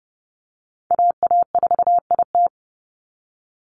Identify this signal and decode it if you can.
Morse code, heard as a keyed tone, about 30 wpm, 700 Hz. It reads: AA4IT
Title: NA SOTA Chaser Callsigns 30wpm